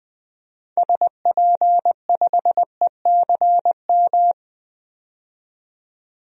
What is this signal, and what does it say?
Morse code, heard as a keyed tone, about 20 wpm, 700 Hz. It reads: SP5ECM
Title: DXCC-100 Call Signs - 6 Characters - 20wpm